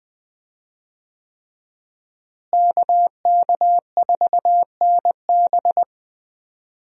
Morse code, heard as a keyed tone, about 20 wpm, 700 Hz. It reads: KK4NB
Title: USA Call Signs - 5 Characters - Repeated 3x - 20wpm